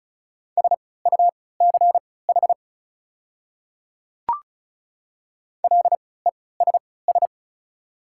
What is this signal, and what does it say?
Morse code, heard as a keyed tone, about 34 wpm, spaced as for 20 wpm, 700 Hz. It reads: SUCH LESS
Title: 3x Character Spacing - Top 500 Words - 35wpm